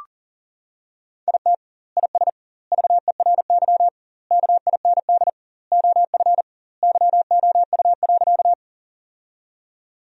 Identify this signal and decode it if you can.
Morse code, heard as a keyed tone, about 40 wpm, 700 Hz. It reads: IT IS VERY KIND OF YOU.
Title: 2x Word-Spacing - Sentences from Top 200 Words - 40wpm